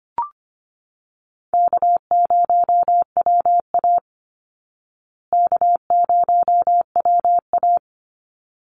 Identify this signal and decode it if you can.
Morse code, heard as a keyed tone, about 25 wpm, 700 Hz. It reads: K0WA K0WA
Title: Call Signs Repeated 3x - 25wpm